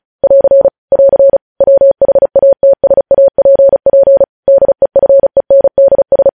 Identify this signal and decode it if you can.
Morse code, heard as a keyed tone, about 36 wpm, 550 Hz. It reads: <AR> <AR> WHATSAPP DEFENDS
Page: news headlines in morse code at 35 wpm